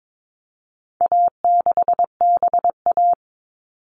Morse code, heard as a keyed tone, about 22 wpm, 700 Hz. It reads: A6BA
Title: All Cut Numbers - 4-Digit 22wpm